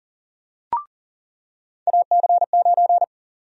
Morse code, heard as a keyed tone, about 39 wpm, 700 Hz. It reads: AC9
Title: USA Call Sign Prefixes - 40wpm